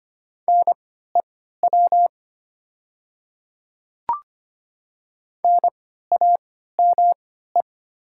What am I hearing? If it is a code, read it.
Morse code, heard as a keyed tone, about 25 wpm, spaced as for 14 wpm, 700 Hz. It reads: NEW NAME
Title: 3x Character Spacing - Top 200 Words - 25wpm